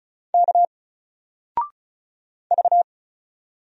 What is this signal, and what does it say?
Morse code, heard as a keyed tone, about 35 wpm, 700 Hz. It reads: K V
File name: Single-Letter-Number-35wpm